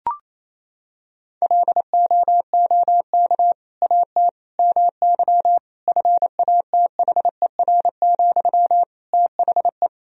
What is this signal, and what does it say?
Morse code, heard as a keyed tone, about 27 wpm, 700 Hz. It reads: LOOK AT MY FATHER, THE
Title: Sentences from Top 1000 Words - Review Part 3 - 28wpm